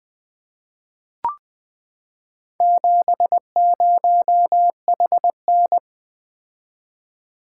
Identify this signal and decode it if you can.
Morse code, heard as a keyed tone, about 20 wpm, 700 Hz. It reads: 70HN
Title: Four Letter-Number Combinations - 20wpm